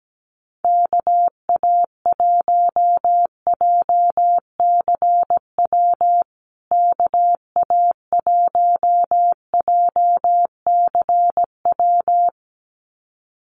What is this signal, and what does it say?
Morse code, heard as a keyed tone, about 17 wpm, 700 Hz. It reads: KA1JCW KA1JCW
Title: USA Call Signs Repeated 2x 17wpm